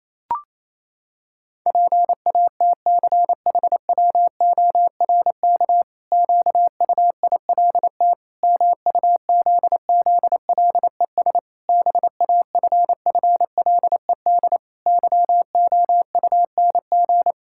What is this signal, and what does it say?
Morse code, heard as a keyed tone, about 28 wpm, 700 Hz. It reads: PATCHWORK QUILT MUZZLES BAFFLED YOUNG
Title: Pangram Challenge v2 - 28wpm